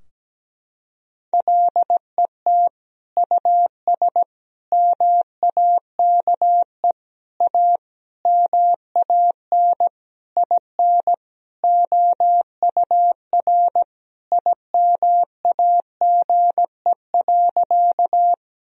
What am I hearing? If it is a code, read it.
Morse code, heard as a keyed tone, about 17 wpm, 700 Hz. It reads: LET US MAKE A MAN IN OUR IMAGE.
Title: Sentences from Top 1000 Words - Review Part 4 - 17wpm